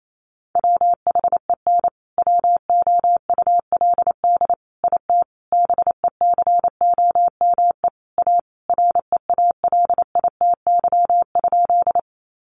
Morse code, heard as a keyed tone, about 28 wpm, 700 Hz. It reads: WHEN WOULD IT BECOME A REALITY?
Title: Sentences from Top 1200 Words 28wpm